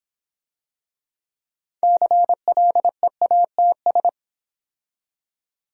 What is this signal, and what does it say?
Morse code, heard as a keyed tone, about 26 wpm, 700 Hz. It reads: CLEATS